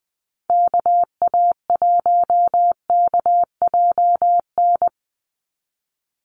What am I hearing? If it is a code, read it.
Morse code, heard as a keyed tone, about 20 wpm, 700 Hz. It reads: KA1KJN
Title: USA Call Signs 20wpm